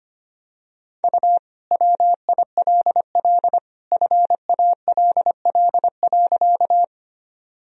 Morse code, heard as a keyed tone, about 25 wpm, 700 Hz. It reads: U WILL FALL.
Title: Sentences from Top 500 Words and Abbreviations 25wpm